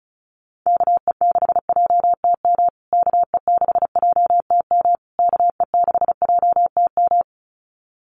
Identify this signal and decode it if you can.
Morse code, heard as a keyed tone, about 34 wpm, 700 Hz. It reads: KE6JTM KE6JTM KE6JTM
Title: USA Call Signs Repeated 3x 35wpm